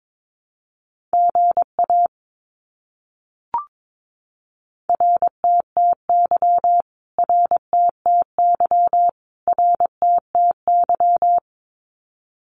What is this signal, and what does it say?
Morse code, heard as a keyed tone, about 22 wpm, 700 Hz. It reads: GA RTTY RTTY RTTY
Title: Abbreviations Repeated 3x - 22wpm